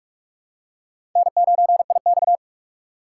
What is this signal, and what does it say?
Morse code, heard as a keyed tone, about 46 wpm, 700 Hz. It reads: N9IX
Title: USA Call Signs - 4 Characters - 3x Character Spacing - 45wpm